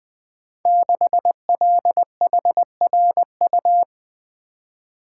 Morse code, heard as a keyed tone, about 20 wpm, 700 Hz. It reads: 6LHRU
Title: Code Groups - 5 Letter:Numbers - 20wpm